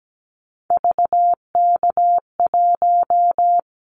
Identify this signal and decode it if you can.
Morse code, heard as a keyed tone, about 17 wpm, 700 Hz. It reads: VK1
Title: Callsign Prefixes for Top 100 DXCC - 17wpm